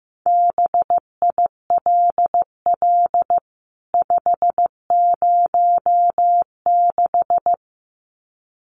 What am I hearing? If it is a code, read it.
Morse code, heard as a keyed tone, about 15 wpm, 700 Hz. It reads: BILL 506
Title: CWT Exchange 15wpm